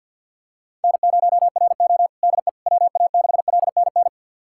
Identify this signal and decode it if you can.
Morse code, heard as a keyed tone, about 51 wpm, 700 Hz. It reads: N0RO DE WA6LNN
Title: USA Callsign DE Callsign 50wpm